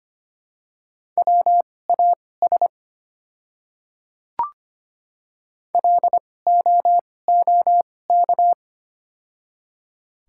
Morse code, heard as a keyed tone, about 25 wpm, spaced as for 18 wpm, 700 Hz. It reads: WAS LOOK
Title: 2x Character Spacing - Top 100 Words - 25wpm